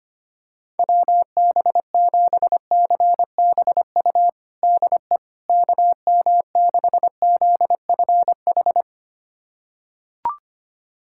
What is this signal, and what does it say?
Morse code, heard as a keyed tone, about 25 wpm, 700 Hz. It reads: WB7CBU DE KM6ZFH
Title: USA Callsign DE Callsign - 6 Characters - 25wpm